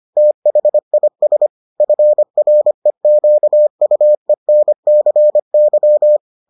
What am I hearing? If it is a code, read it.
Morse code, heard as a keyed tone, about 25 wpm, 600 Hz. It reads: THIS FREQUENCY